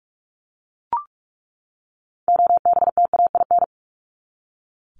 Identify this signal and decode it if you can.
Morse code, heard as a keyed tone, about 44 wpm, 700 Hz. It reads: OBTAIN
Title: ED Suffix - Top 100 Words with Base Words 45wpm